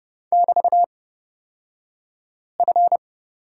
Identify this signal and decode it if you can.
Morse code, heard as a keyed tone, about 30 wpm, spaced as for 5 wpm, 700 Hz. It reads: <BT>F
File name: Lesson 36d - BT - Two Characters 30wpm